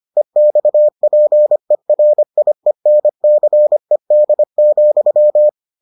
Morse code, heard as a keyed tone, about 25 wpm, 600 Hz. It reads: EXPERIENCED,